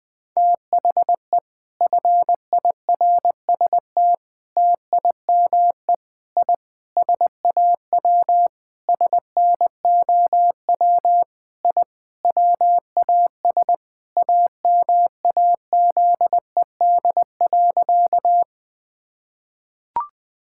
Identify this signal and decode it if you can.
Morse code, heard as a keyed tone, about 20 wpm, 700 Hz. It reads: THE FIRST TIME I SAW SNOW I WAS AMAZED.
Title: Sentences with 4-Word Phrases - 20wpm